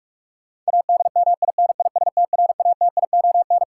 Morse code, heard as a keyed tone, about 48 wpm, 700 Hz. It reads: ADMINISTRATION
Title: Sets of 5 Words - New Word Drill - 45wpm